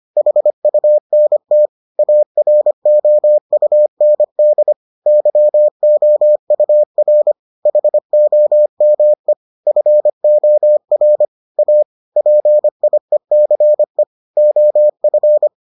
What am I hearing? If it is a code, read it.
Morse code, heard as a keyed tone, about 25 wpm, 600 Hz. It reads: HUNT AROUND YOUR HOME FOR A PIECE OF